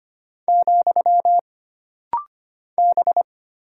Morse code, heard as a keyed tone, about 25 wpm, 700 Hz. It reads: , B
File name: Single Character (Letter:Number:Punctuation:Prosign) - Rapid-Fire - 25wpm